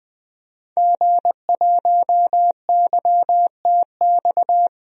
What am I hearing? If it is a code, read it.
Morse code, heard as a keyed tone, about 20 wpm, 700 Hz. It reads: G1YTX
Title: DXCC-100 Call Signs - 5 Characters - 20wpm